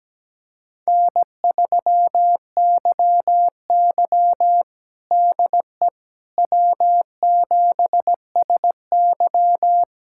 Morse code, heard as a keyed tone, about 17 wpm, 700 Hz. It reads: N3YY DE W7SY
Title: USA Callsign DE Callsign - 4 Characters - 17wpm